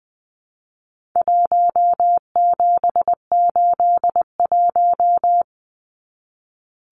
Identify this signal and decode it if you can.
Morse code, heard as a keyed tone, about 20 wpm, 700 Hz. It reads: 1781
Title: Numbers - Contest Style - 4-Digit 20wpm